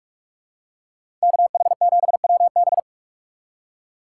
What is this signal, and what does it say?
Morse code, heard as a keyed tone, about 46 wpm, 700 Hz. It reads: KH7WB